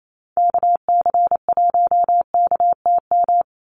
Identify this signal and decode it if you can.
Morse code, heard as a keyed tone, about 28 wpm, 700 Hz. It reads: KC1KTM
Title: USA Call Signs 28wpm